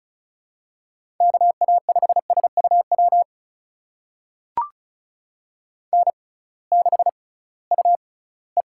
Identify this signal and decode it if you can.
Morse code, heard as a keyed tone, about 36 wpm, spaced as for 21 wpm, 700 Hz. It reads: KA5SUW N6UE
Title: USA Call Signs - 6x Character Spacing - 35wpm